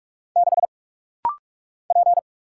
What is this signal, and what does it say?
Morse code, heard as a keyed tone, about 46 wpm, 700 Hz. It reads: 6 P
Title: Single Letter-Number Alt - Rapid-Fire 45wpm